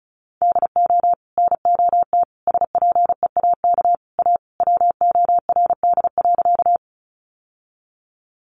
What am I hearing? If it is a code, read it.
Morse code, heard as a keyed tone, about 34 wpm, 700 Hz. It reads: DO NOT SPEAK A WORD.
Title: Sentences from Top 1100 Words 35wpm